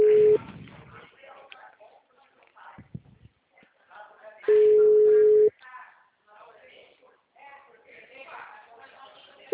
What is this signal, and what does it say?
Ringback tone (Europe, 425 Hz), with speech.
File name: out- -2010- - - .56